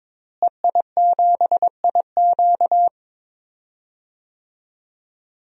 Morse code, heard as a keyed tone, about 22 wpm, 700 Hz. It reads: EI7IQ
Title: DXCC-100 Call Signs - 5 Characters - Repeated 3x - 22wpm